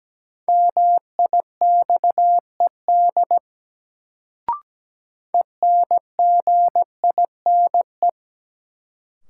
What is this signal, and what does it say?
Morse code, heard as a keyed tone, about 17 wpm, 700 Hz. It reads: MIXED ENGINE
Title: Sentences from Top 1500 Words - New Word Drill - 17wpm